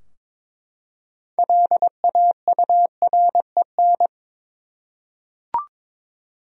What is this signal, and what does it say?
Morse code, heard as a keyed tone, about 22 wpm, 700 Hz. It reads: LAUREN
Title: USA Common Names - 22wpm